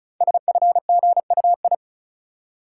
Morse code, heard as a keyed tone, about 36 wpm, 700 Hz. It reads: SFGUI
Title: Code Groups - 5 Letter:Numbers - 35wpm